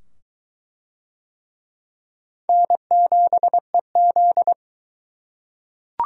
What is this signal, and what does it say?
Morse code, heard as a keyed tone, about 23 wpm, 700 Hz. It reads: N7EZ